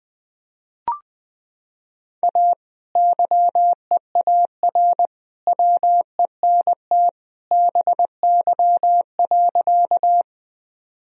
Morse code, heard as a keyed tone, about 21 wpm, 700 Hz. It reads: A YEAR WENT BY.
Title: Sentences from Top 1000 Words - Review Part 1 - 20wpm